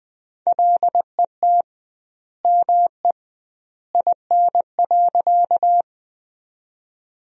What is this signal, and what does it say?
Morse code, heard as a keyed tone, about 21 wpm, 700 Hz. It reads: LET ME IN.
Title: 2x Word-Spacing - Sentences from Top 300 Words - 20wpm